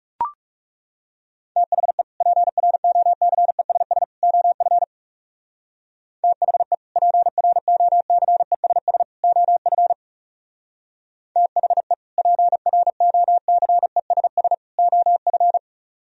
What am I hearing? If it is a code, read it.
Morse code, heard as a keyed tone, about 37 wpm, 700 Hz. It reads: THE PROCESS OF THE PROCESS OF THE PROCESS OF
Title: Speed-Racing - Sets of 3 Words 30wpm